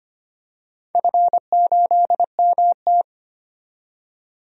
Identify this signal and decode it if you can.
Morse code, heard as a keyed tone, about 25 wpm, 700 Hz. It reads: F8MT
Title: DXCC-100 Call Signs - 4 Characters - 25wpm